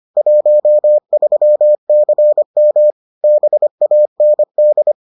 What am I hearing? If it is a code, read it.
Morse code, heard as a keyed tone, about 25 wpm, 600 Hz. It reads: 13CM BAND